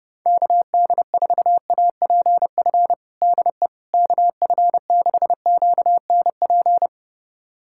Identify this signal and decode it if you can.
Morse code, heard as a keyed tone, about 30 wpm, 700 Hz. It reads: KD4APF DE KF6QNP
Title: USA Callsign DE Callsign - 6 Characters - 30wpm